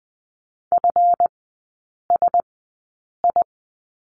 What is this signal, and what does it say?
Morse code, heard as a keyed tone, about 20 wpm, spaced as for 8 wpm, 690 Hz. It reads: FSI
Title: 191 Lesson 31e - Letter v - Three Characters 20wpm